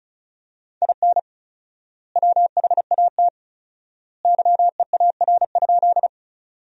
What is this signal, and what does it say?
Morse code, heard as a keyed tone, about 34 wpm, 700 Hz. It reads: IN WHAT YEAR?
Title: 4x Word-Spacing - Sentences from Top 200 Words - 35wpm